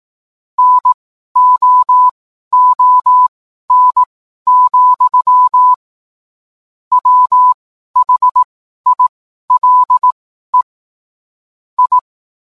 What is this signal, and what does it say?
Morse code, heard as a keyed tone, about 18 wpm, spaced as for 12 wpm, 1000 Hz. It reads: NOON, WHILE I